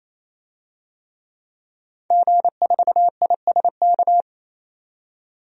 Morse code, heard as a keyed tone, about 28 wpm, 700 Hz. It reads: G4ISK
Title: DXCC-100 Call Signs - 5 Characters - 28wpm